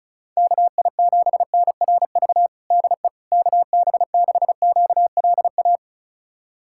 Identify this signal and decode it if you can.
Morse code, heard as a keyed tone, about 34 wpm, 700 Hz. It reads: KI7NRV DE KB6QLA
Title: USA Callsign DE Callsign 35wpm